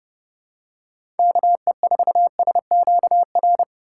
Morse code, heard as a keyed tone, about 31 wpm, 700 Hz. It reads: KE4SQR
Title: USA Call Signs - 6 Characters - 2x Character Spacing - 30wpm